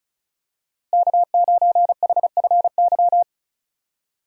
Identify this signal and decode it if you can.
Morse code, heard as a keyed tone, about 34 wpm, 700 Hz. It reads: K9HFY
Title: USA Call Signs - 5 Characters - Repeated 2x - 35wpm